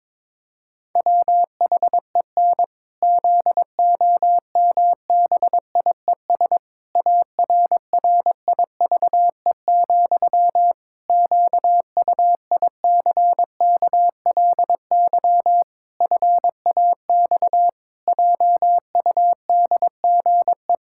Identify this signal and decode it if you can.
Morse code, heard as a keyed tone, about 22 wpm, 700 Hz. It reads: WHEN ZOMBIES ARRIVE, QUICKLY FAX JUDGE
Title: Pangram Challenge v2 - 22wpm